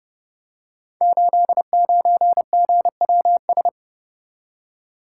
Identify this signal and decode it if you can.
Morse code, heard as a keyed tone, about 29 wpm, 700 Hz. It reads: 89GWS
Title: Code Groups - 5 Letter:Numbers - 30wpm